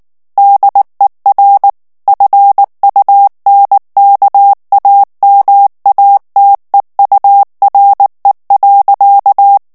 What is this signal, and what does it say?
Morse code, heard as a keyed tone, about 19 wpm, 800 Hz. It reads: DER FUNKAMATEURE.
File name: morse mp3 / ALLEMAND texte clair 2 vitesse 18 mots minute